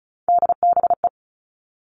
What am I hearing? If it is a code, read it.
Morse code, heard as a keyed tone, about 34 wpm, 700 Hz. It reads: DBE
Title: All Cut Numbers - 3-Digit 35wpm